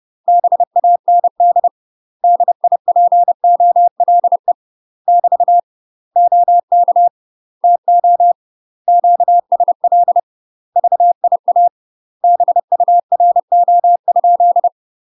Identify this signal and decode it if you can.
Morse code, heard as a keyed tone, about 30 wpm, 700 Hz. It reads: BAND DIPOLE <BT> OK TO QSL VIA BURO?
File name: UK MASTER QSOs TRAINING/30WPM